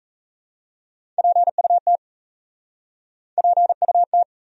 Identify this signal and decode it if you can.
Morse code, heard as a keyed tone, about 37 wpm, 700 Hz. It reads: PUT PUT